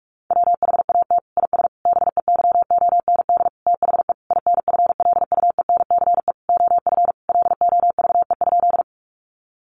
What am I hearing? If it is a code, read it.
Morse code, heard as a keyed tone, about 44 wpm, 700 Hz. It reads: WHAT IS BEYOND THE INFLUENCE OF LOVE?
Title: Sentences from Top 1100 Words 45wpm